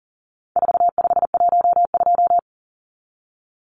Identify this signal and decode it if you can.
Morse code, heard as a keyed tone, about 40 wpm, 700 Hz. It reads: 4512
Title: Numbers - 4-Digit 40wpm